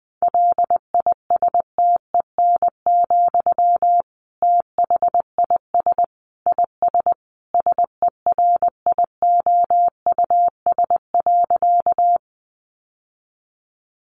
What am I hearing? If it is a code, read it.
Morse code, heard as a keyed tone, about 20 wpm, 700 Hz. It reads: LISTEN, THIS IS SERIOUS.
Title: Sentences from Top 1100 Words 20wpm